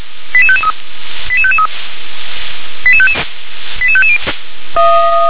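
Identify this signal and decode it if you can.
Five-tone selective call (ZVEI), heard as quick runs of five tones, tones of about 70 ms. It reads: 88553 88553 88550
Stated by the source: ZVEI five-tone call received on an SDR.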